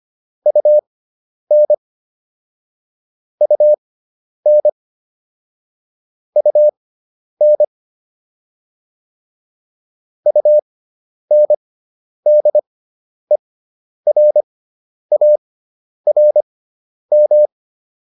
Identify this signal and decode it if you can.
Morse code, heard as a keyed tone, about 25 wpm, spaced as for 10 wpm, 600 Hz. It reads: UN UN UN UNDERARM